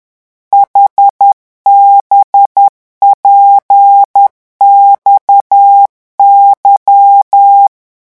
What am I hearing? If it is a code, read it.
Morse code, heard as a keyed tone, about 10 wpm, 800 Hz. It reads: HBPXY
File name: morse mp3 / revision5-vitesse 10 mots